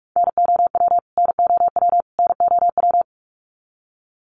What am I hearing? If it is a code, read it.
Morse code, heard as a keyed tone, about 44 wpm, 700 Hz. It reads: NOW NOW NOW
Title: Top 200 Words Repeated 45wpm